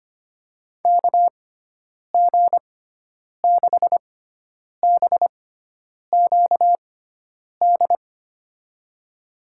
Morse code, heard as a keyed tone, about 25 wpm, spaced as for 9 wpm, 700 Hz. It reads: KG6BQD
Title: USA Call Signs - 6x Character Spacing - 25wpm